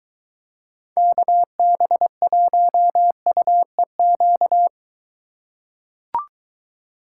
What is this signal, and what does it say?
Morse code, heard as a keyed tone, about 23 wpm, 700 Hz. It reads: KB1UEQ